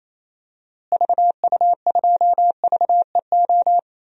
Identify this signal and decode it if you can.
Morse code, heard as a keyed tone, about 28 wpm, 700 Hz. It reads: VU2VEO
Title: DXCC-100 Call Signs - 6 Characters - Repeated 2x - 28wpm